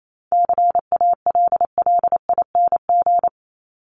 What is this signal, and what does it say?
Morse code, heard as a keyed tone, about 28 wpm, 700 Hz. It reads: CALLING
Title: ING Suffix - Top 100 Words with Base Words 28wpm